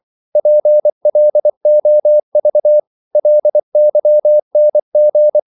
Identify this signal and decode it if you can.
Morse code, heard as a keyed tone, about 24 wpm, 600 Hz. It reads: PLOV LYNG
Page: Korte danske ord 24wpm